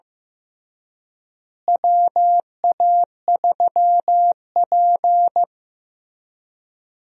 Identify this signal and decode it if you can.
Morse code, heard as a keyed tone, about 16 wpm, 700 Hz. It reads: WA3P
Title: USA Call Signs - 4 Characters - Repeated 3x - 15wpm